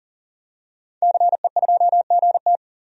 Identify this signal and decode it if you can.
Morse code, heard as a keyed tone, about 40 wpm, 700 Hz. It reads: CE2GT
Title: DXCC-100 Call Signs - 5 Characters - 40wpm